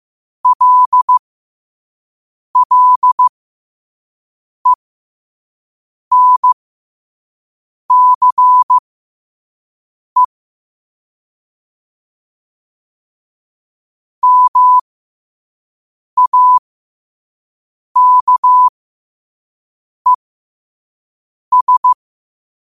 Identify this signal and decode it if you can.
Morse code, heard as a keyed tone, about 15 wpm, spaced as for 5 wpm, 1000 Hz. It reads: LLENCE MAKES
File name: qotdfiles / 5wpm